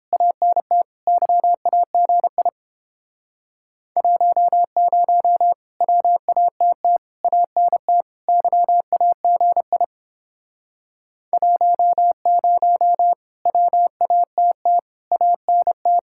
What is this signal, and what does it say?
Morse code, heard as a keyed tone, about 29 wpm, 700 Hz. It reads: ANT YAGI 10 WATT ANT YAGI 10 WATT ANT